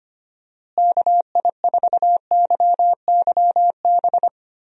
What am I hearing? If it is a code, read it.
Morse code, heard as a keyed tone, about 25 wpm, 700 Hz. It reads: KI4YYB
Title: USA Call Signs - 6 Characters - Repeated 3x - 25wpm